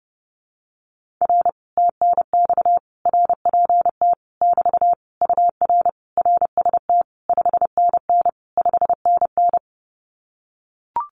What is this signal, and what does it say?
Morse code, heard as a keyed tone, about 29 wpm, 700 Hz. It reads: R TNX RPT <BT> UR RST 5NN 5NN
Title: Common QSO Phrases Encore 30wpm